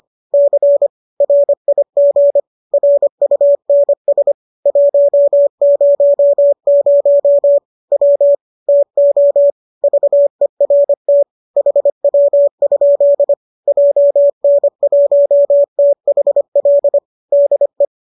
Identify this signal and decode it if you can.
Morse code, heard as a keyed tone, about 25 wpm, 570 Hz. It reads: C RIG RUNS 100 W TO VERT HW? JN1THL DE
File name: qso205 25